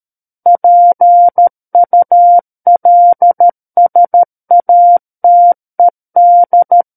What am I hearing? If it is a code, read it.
Morse code, heard as a keyed tone, about 13 wpm, 700 Hz. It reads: PULSATED
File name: Book One / 04-Chapter 4 - 13 WPM